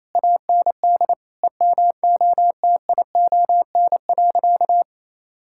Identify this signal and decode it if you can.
Morse code, heard as a keyed tone, about 28 wpm, 700 Hz. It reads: AND EMOTION.
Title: Sentences from Top 1600 Words 28wpm